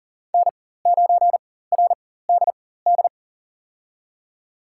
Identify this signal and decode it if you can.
Morse code, heard as a keyed tone, about 40 wpm, spaced as for 19 wpm, 700 Hz. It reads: N9RDD
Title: USA Call Signs - 5 Characters - 4x Character Spacing - 40wpm